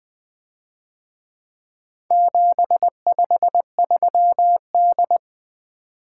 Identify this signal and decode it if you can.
Morse code, heard as a keyed tone, about 21 wpm, 700 Hz. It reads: 753D
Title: Speed-Racing - Four Letter-Number Combinations - 20wpm